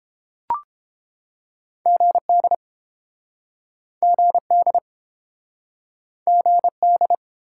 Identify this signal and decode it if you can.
Morse code, heard as a keyed tone, about 31 wpm, 700 Hz. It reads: GD GD GD